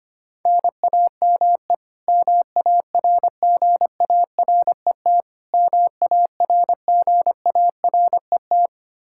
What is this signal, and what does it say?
Morse code, heard as a keyed tone, about 25 wpm, 700 Hz. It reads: NAME MARGARET MARGARET
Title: K1USN SST QSO Practice 25wpm